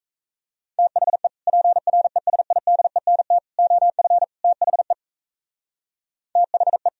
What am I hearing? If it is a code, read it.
Morse code, heard as a keyed tone, about 35 wpm, 700 Hz. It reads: THE PRESIDENT OF THE THE
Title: Speed-Racing - Sets of 4 Words 28wpm